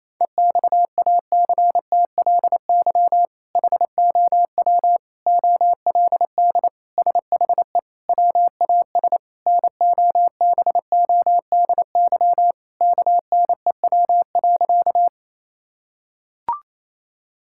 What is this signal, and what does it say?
Morse code, heard as a keyed tone, about 28 wpm, 700 Hz. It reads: EXACTLY HOW OLD SHE WAS NOBODY KNEW.
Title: Sentences from Top 1600 Words 28wpm